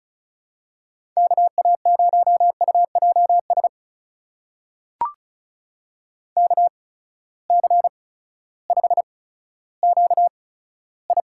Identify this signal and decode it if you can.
Morse code, heard as a keyed tone, about 36 wpm, spaced as for 16 wpm, 700 Hz. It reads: KA0UJS KC5QI
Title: USA Call Signs - 6 Characters - 8x Character Spacing - 35wpm